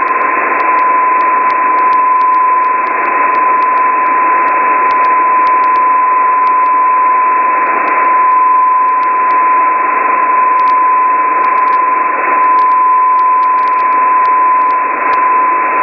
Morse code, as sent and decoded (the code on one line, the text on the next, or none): none